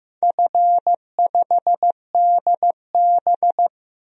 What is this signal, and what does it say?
Morse code, heard as a keyed tone, about 15 wpm, 700 Hz. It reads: F5DB
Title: DXCC-100 Call Signs - 4 Characters - Repeated 2x - 15wpm